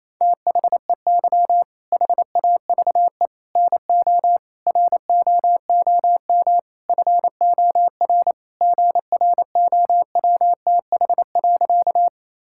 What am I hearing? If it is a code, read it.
Morse code, heard as a keyed tone, about 28 wpm, 700 Hz. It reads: THEY HAVE NO ROOM FOR GROWTH.
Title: Sentences from Top 1000 Words - Review Part 5 - 28wpm